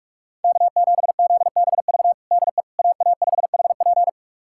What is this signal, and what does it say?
Morse code, heard as a keyed tone, about 43 wpm, 700 Hz. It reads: K7ZBV DE AA5HP
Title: USA Callsign DE Callsign - 5 Characters - 45wpm